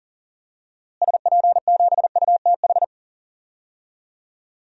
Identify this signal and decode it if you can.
Morse code, heard as a keyed tone, about 40 wpm, 700 Hz. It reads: SP7UTH
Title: DXCC-100 Call Signs - 6 Characters - Repeated 2x - 40wpm